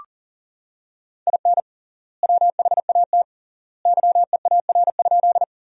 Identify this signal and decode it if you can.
Morse code, heard as a keyed tone, about 37 wpm, 700 Hz. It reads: IN WHAT YEAR?
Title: 3x Word-Spacing - Sentences from Top 200 Words - 40wpm